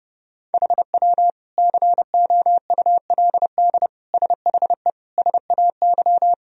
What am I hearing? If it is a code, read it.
Morse code, heard as a keyed tone, about 30 wpm, 700 Hz. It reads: HW COULD SHE SAY
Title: Sentences from Top 200 Words and Abbreviations 30wpm